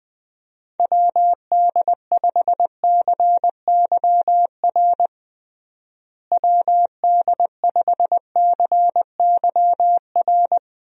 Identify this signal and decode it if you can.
Morse code, heard as a keyed tone, about 20 wpm, 700 Hz. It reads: WD5CYR WD5CYR
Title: USA Call Signs - 6 Characters - Repeated 3x - 20wpm